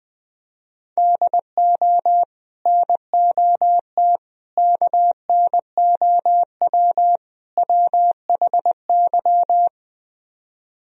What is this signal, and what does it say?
Morse code, heard as a keyed tone, about 20 wpm, 700 Hz. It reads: DO NOT KNOW WHY
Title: Speed-Racing - Sets of 4 Words 20wpm